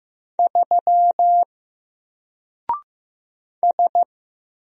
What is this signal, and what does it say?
Morse code, heard as a keyed tone, about 15 wpm, 700 Hz. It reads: 3 S
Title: Single Character (Letter:Number:Punctuation:Prosign) 15wpm